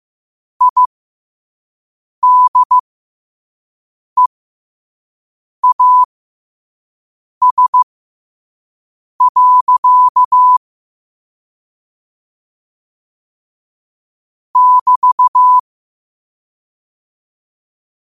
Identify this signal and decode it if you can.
Morse code, heard as a keyed tone, about 15 wpm, spaced as for 5 wpm, 1000 Hz. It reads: IDEAS. <BT>
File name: qotdfiles / 5wpm